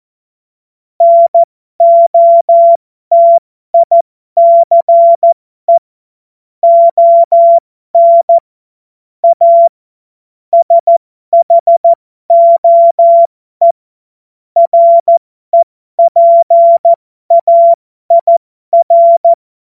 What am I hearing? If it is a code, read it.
Morse code, heard as a keyed tone, about 14 wpm, spaced as for 12 wpm, 680 Hz. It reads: NOTICE ON A SHOE REPAIR
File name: Morsekurs / Witze